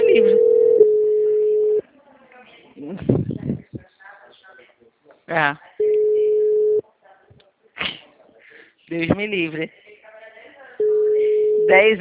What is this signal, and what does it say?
Ringback tone (Europe, 425 Hz), with speech.
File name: out- -2032- - - .57052